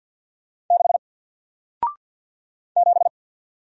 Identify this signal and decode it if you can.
Morse code, heard as a keyed tone, about 50 wpm, 700 Hz. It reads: 6 7
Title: Single Character (Letter:Number:Punctuation:Prosign) Alt 50wpm